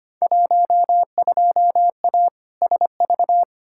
Morse code, heard as a keyed tone, about 25 wpm, 700 Hz. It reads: 12A SV
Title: Field Day Exchanges Encore - 25wpm